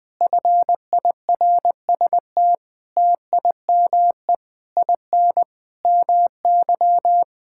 Morse code, heard as a keyed tone, about 20 wpm, 700 Hz. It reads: FIRST TIME IN MY
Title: Sentences with 7-Word Phrases - 20wpm